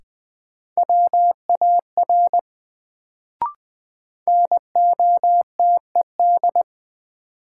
Morse code, heard as a keyed tone, about 20 wpm, 700 Hz. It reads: WAR NOTED
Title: Sets of 5 Words - New Word Drill - 20wpm